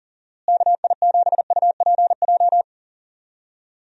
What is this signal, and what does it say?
Morse code, heard as a keyed tone, about 40 wpm, 700 Hz. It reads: KI7UPJ
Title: USA Call Signs - 6 Characters - Repeated 2x - 40wpm